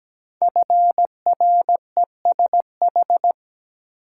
Morse code, heard as a keyed tone, about 17 wpm, 700 Hz. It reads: FRESH
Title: Sentences from Top 1200 Words - New Word Drill Alt - 17wpm